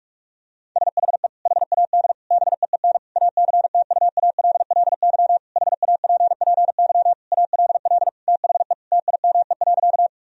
Morse code, heard as a keyed tone, about 46 wpm, 700 Hz. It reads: SHE HAD BEEN ACTUALLY HAPPY ALL THE TIME.
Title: Sentences from Top 1000 Words - Review Part 4 - 45wpm